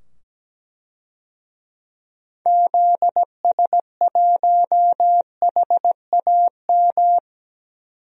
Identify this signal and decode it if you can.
Morse code, heard as a keyed tone, about 17 wpm, 700 Hz. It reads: ZS1HAM
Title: DXCC-100 Call Signs - 6 Characters - 17wpm